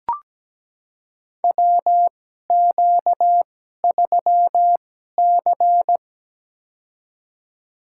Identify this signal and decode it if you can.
Morse code, heard as a keyed tone, about 17 wpm, spaced as for 12 wpm, 700 Hz. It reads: WQ3C
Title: USA Call Signs - 2x Character Spacing - 17wpm